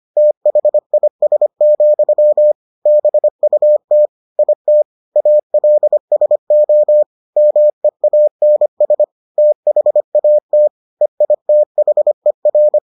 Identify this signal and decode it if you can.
Morse code, heard as a keyed tone, about 25 wpm, 600 Hz. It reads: THIS, BUT IT ALSO MEANS THAT EITHER